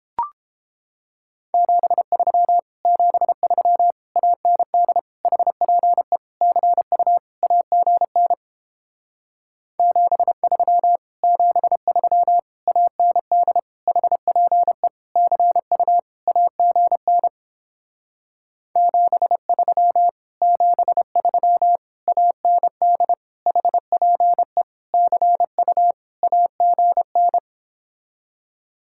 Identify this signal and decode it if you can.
Morse code, heard as a keyed tone, about 31 wpm, 700 Hz. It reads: 73 73 AND HPE CU AGN 73 73 AND HPE CU AGN 73 73 AND HPE CU AGN